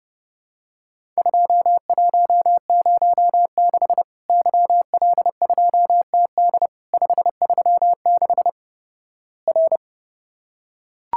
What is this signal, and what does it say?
Morse code, heard as a keyed tone, about 30 wpm, 700 Hz. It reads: 2106 YL2TD 536 R
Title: Worked All Europe DX Contest - QTC Exchange - 30wpm